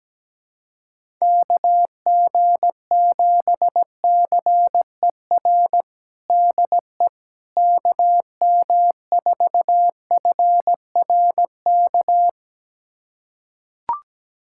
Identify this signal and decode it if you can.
Morse code, heard as a keyed tone, about 17 wpm, 700 Hz. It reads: KG7CER DE KM4FRK
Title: USA Callsign DE Callsign 17wpm